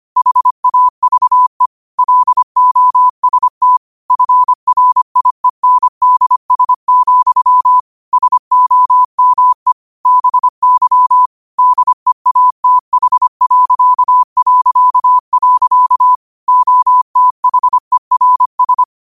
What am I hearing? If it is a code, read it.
Morse code, heard as a keyed tone, about 25 wpm, 1000 Hz. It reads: HAVE LOST FRIENDS, SOME BY DEATH... OTHERS
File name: qotdfiles / 25wpm